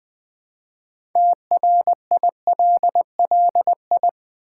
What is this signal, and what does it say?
Morse code, heard as a keyed tone, about 20 wpm, 700 Hz. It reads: TRILLI
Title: Build A Word - 3 Letters At A Time - 8-Letter Words - 20wpm